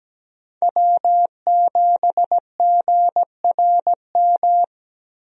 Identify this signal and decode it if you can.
Morse code, heard as a keyed tone, about 17 wpm, 700 Hz. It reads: W7GRM
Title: USA Call Signs - 5 Characters - Repeated 2x - 17wpm